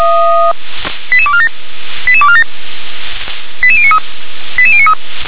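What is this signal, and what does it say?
Five-tone selective call (ZVEI), heard as quick runs of five tones, tones of about 70 ms. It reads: 88247 88247 88093 88093
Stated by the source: ZVEI five-tone selective call received on an SDR.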